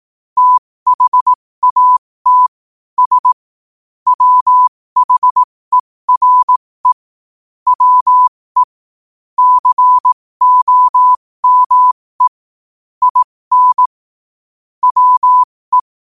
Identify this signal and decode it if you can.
Morse code, heard as a keyed tone, about 18 wpm, spaced as for 15 wpm, 1000 Hz. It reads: THAT S WHERE WE COME IN WE
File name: qotdfiles / 15wpm